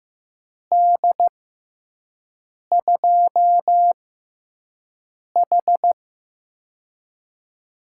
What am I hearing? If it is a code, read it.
Morse code, heard as a keyed tone, about 15 wpm, spaced as for 5 wpm, 700 Hz. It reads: D2H
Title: USA Call Signs - 4 Characters - 6x Character Spacing - 15wpm